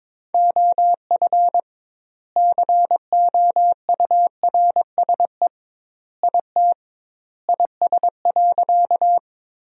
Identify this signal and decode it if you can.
Morse code, heard as a keyed tone, about 22 wpm, 700 Hz. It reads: OF COURSE IT IS.
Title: 2x Word-Spacing - Sentences from Top 500 Words - 22wpm